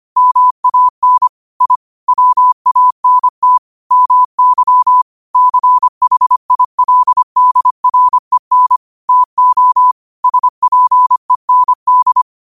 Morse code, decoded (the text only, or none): MAN I WANT MY CHILDREN TO SPEND